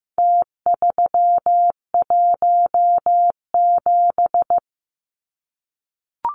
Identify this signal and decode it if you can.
Morse code, heard as a keyed tone, about 15 wpm, 700 Hz. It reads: T317
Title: Numbers - Contest Style - 4-Digit 15wpm